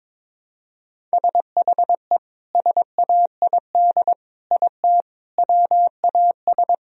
Morse code, heard as a keyed tone, about 22 wpm, 700 Hz. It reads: SHE SAID IT WAS
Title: Sentences from Top 1000 Words - Review Part 2 - 22wpm